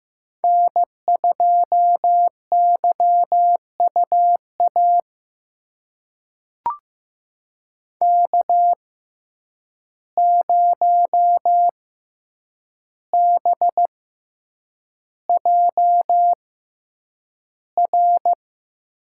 Morse code, heard as a keyed tone, about 15 wpm, spaced as for 8 wpm, 700 Hz. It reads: N2YUA K0BJR
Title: USA Call Signs - 5 Characters - 6x Character Spacing - 15wpm